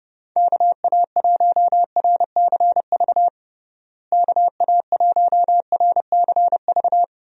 Morse code, heard as a keyed tone, about 30 wpm, 700 Hz. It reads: KA1RCV KA1RCV
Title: USA Call Signs - 6 Characters - Repeated 2x - 30wpm